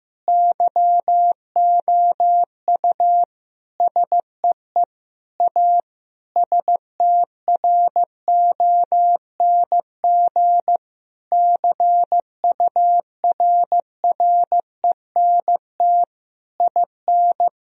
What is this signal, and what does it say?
Morse code, heard as a keyed tone, about 15 wpm, 700 Hz. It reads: YOU SEE A STRONG CURRENT IN
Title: Sentences from Top 1000 Words - Review Part 4 - 15wpm